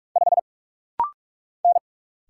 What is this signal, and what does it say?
Morse code, heard as a keyed tone, about 44 wpm, 700 Hz. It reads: 5 N
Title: Single Letter-Number Alt - Rapid-Fire 45wpm